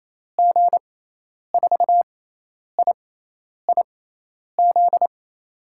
Morse code, heard as a keyed tone, about 28 wpm, spaced as for 10 wpm, 700 Hz. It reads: G4IIZ
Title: USA Call Signs - 6 Characters - 6x Character Spacing - 28wpm